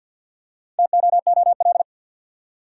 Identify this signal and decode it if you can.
Morse code, heard as a keyed tone, about 53 wpm, 700 Hz. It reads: TOOL
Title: Four Letter Words - 50wpm